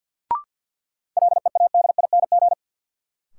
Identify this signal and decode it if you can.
Morse code, heard as a keyed tone, about 52 wpm, 700 Hz. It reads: LEADING
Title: Sentences from Top 1100 Words - New Word Drill Alt - 50wpm